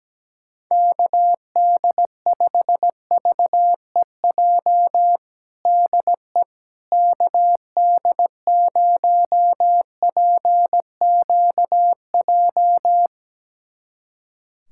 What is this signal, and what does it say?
Morse code, heard as a keyed tone, about 17 wpm, 700 Hz. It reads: KD5VEJ DE KD0PQJ
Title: USA Callsign DE Callsign - 6 Characters - 17wpm